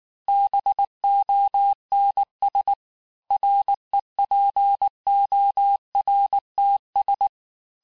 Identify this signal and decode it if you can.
Morse code, heard as a keyed tone, about 19 wpm, 800 Hz. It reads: BONS REPORTS
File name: cours-du-lundi files / cours-1583-fin-QSO-18-mpm